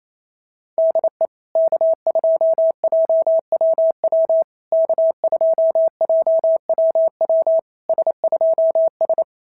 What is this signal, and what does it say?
Morse code, heard as a keyed tone, about 28 wpm, 650 Hz. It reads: DE K2JWW K2JWW S2S
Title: SOTA QSO Practice 28wpm